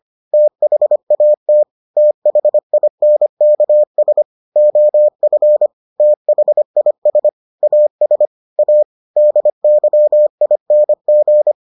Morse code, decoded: THAT THINKS OF THIS AS A DYING

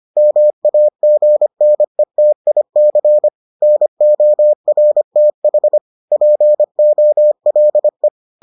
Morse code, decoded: MAGNETIC NORTH POLE

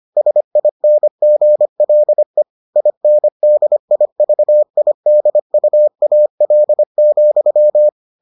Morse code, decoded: SINGLE INDIVIDUAL,

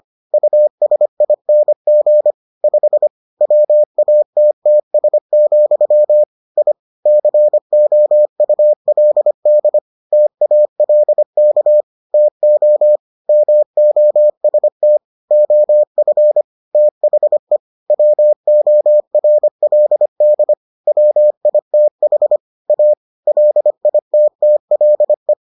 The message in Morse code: USING 5 WATTS, I COULD TALK TO MOST OF THE WORLD WITH A LITTLE

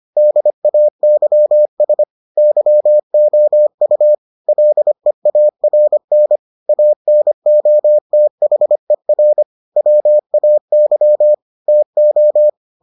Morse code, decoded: DAYS YOU LEARN ANOTHER WAY TO

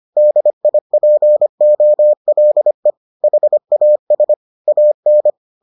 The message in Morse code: DIPOLE HAS AN

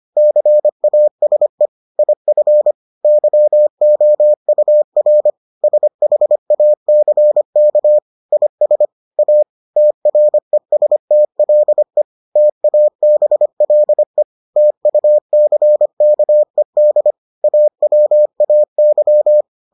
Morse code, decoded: CASE IF YOUR SHACK IS A TRESTLE TABLE TUCKED AWAY

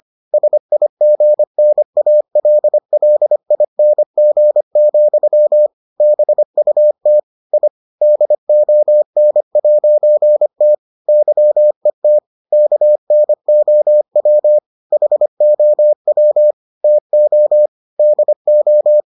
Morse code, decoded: SIGNALLING, BUT I DON'T YET KNOW HOW TO DO

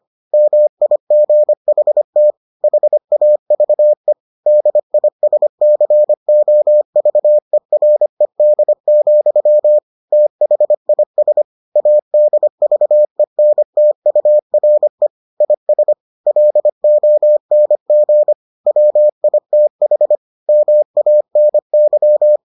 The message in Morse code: MIGHT HAVE DISCOVERED, THIS ADVENTURE IS LONG WITH MANY

